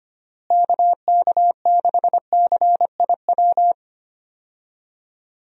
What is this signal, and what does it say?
Morse code, heard as a keyed tone, about 25 wpm, 700 Hz. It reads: KK6CIW
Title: USA Call Signs - 6 Characters - Repeated 2x - 25wpm